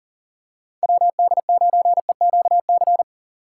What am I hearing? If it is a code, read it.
Morse code, heard as a keyed tone, about 39 wpm, 700 Hz. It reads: WD9EQC
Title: USA Call Signs - 6 Characters - 3x Character Spacing - 40wpm